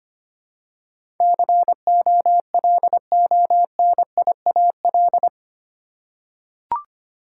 Morse code, heard as a keyed tone, about 25 wpm, 700 Hz. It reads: COLONIAL
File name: 8 Letter Words - First 3 Letters Given - 25wpm